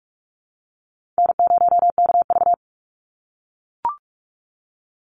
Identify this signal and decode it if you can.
Morse code, heard as a keyed tone, about 44 wpm, 700 Hz. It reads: N0KV
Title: Call Signs - 45wpm